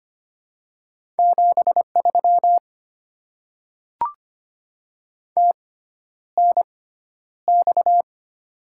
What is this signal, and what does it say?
Morse code, heard as a keyed tone, about 25 wpm, spaced as for 11 wpm, 700 Hz. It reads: 73 TNX
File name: Top 90 QSO Elements - 6x Character Spacing - 25wpm